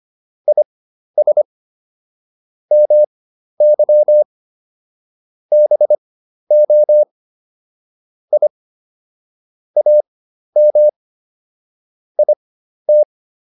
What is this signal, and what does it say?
Morse code, heard as a keyed tone, about 25 wpm, spaced as for 12 wpm, 600 Hz. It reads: IS MY BO I AM IT